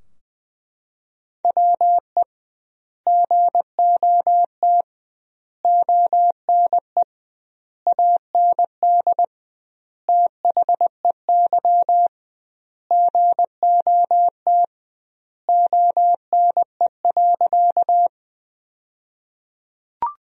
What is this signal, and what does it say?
Morse code, heard as a keyed tone, about 20 wpm, 700 Hz. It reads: WE GOT ONE AND THEY GOT ONE.
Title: 2x Word-Spacing - Sentences from Top 300 Words - 20wpm